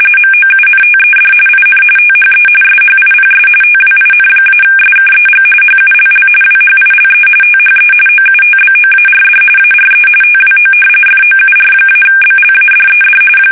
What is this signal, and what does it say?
RTTY, about 45.45 baud. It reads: MOUNTED IN THE SHIPS TWIN-SKEG HULL WHILE THE OUTBOARD ARE TWO FOUR-BLADED
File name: fsk850 / battleshipnj